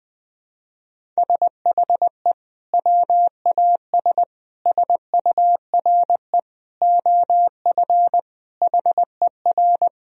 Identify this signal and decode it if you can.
Morse code, heard as a keyed tone, about 21 wpm, 700 Hz. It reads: SHE WAS SURE OF HER
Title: Sentences from Top 1000 Words - Review Part 1 - 20wpm